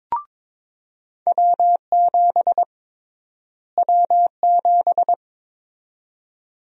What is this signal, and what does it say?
Morse code, heard as a keyed tone, about 22 wpm, 700 Hz. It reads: W7 W7
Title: USA Call Sign Prefixes - Repeated 2x - 22wpm